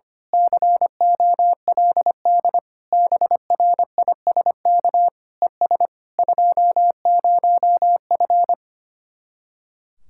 Morse code, heard as a keyed tone, about 24 wpm, 700 Hz. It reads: COLD BRISK ES 20F
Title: Speed-Racing Common QSO Phrases - 25wpm